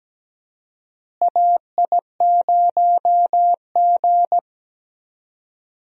Morse code, heard as a keyed tone, about 17 wpm, 700 Hz. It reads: AI0G
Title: USA Call Signs - 4 Characters - Repeated 3x - 17wpm